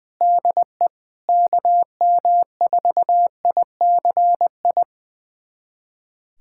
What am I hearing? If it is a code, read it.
Morse code, heard as a keyed tone, about 20 wpm, 700 Hz. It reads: DE KM4ICI
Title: USA Callsign DE Callsign 20wpm